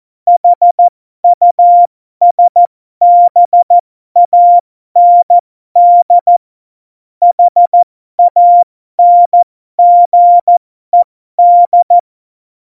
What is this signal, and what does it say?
Morse code, heard as a keyed tone, about 14 wpm, spaced as for 12 wpm, 700 Hz. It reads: HUSBAND HANGED
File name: Morsekurs / Witze